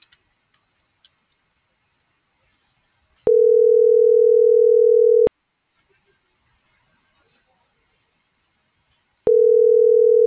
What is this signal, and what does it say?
Ringback tone (North America), 440 + 480 Hz.